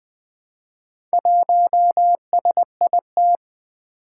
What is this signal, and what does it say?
Morse code, heard as a keyed tone, about 20 wpm, 700 Hz. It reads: 1SIT
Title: Four Challenging Letter-Number Combinations 20wpm